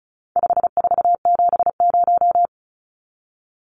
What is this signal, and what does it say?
Morse code, heard as a keyed tone, about 34 wpm, 700 Hz. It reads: 5470
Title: Numbers - 4-Digit 35wpm